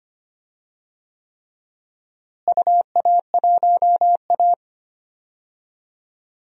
Morse code, heard as a keyed tone, about 25 wpm, 700 Hz. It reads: UA1A
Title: DXCC-100 Call Signs - 4 Characters - Repeated 3x - 25wpm